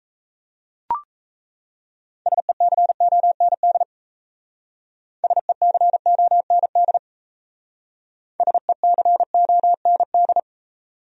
Morse code, heard as a keyed tone, about 35 wpm, 700 Hz. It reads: SECOND SECOND SECOND